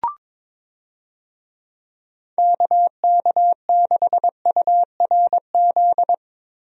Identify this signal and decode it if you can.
Morse code, heard as a keyed tone, about 22 wpm, 700 Hz. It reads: KK6URZ
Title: USA Call Signs - 6 Characters - Repeated 3x - 22wpm